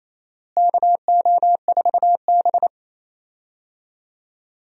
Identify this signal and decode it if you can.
Morse code, heard as a keyed tone, about 28 wpm, 700 Hz. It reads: KO4B
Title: USA Call Signs - 4 Characters - Repeated 3x - 28wpm